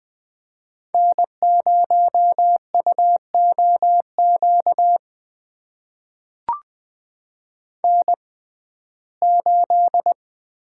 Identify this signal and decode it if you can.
Morse code, heard as a keyed tone, about 20 wpm, spaced as for 15 wpm, 700 Hz. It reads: N0UOQ N8
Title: USA Call Signs - 5 Characters - 6x Character Spacing - 20wpm